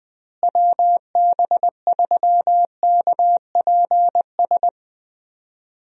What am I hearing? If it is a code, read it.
Morse code, heard as a keyed tone, about 21 wpm, 700 Hz. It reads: WB3KPS
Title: USA Call Signs - 6 Characters - Repeated 2x - 20wpm